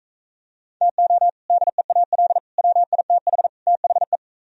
Morse code, heard as a keyed tone, about 44 wpm, 700 Hz. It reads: TO DEAL WITH THE